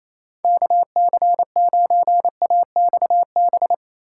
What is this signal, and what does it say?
Morse code, heard as a keyed tone, about 38 wpm, spaced as for 33 wpm, 700 Hz. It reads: KC9AXB
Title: Speed-Racing Common QSO Phrases Encore - 28wpm